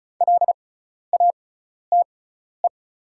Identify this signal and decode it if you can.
Morse code, heard as a keyed tone, about 34 wpm, spaced as for 12 wpm, 690 Hz. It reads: LATE
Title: 6x Character Spacing - Top 300 Words - 35wpm